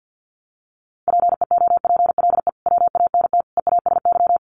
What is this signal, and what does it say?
Morse code, heard as a keyed tone, about 50 wpm, 700 Hz. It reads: PEOPLE WANT EASY
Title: Sentences from Top 1500 Words 50wpm